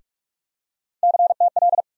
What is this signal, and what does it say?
Morse code, heard as a keyed tone, about 43 wpm, 700 Hz. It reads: CTL
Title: USA Call Sign Postfixes - 3 Letters 45wpm